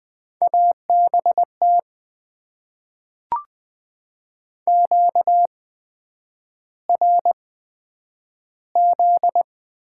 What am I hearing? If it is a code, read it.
Morse code, heard as a keyed tone, about 20 wpm, spaced as for 9 wpm, 700 Hz. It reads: ABT QRZ
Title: Top 90 QSO Elements - 8x Character Spacing - 20wpm